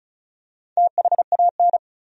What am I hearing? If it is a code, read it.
Morse code, heard as a keyed tone, about 36 wpm, 700 Hz. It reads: THAN
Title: Top 100 Words 35wpm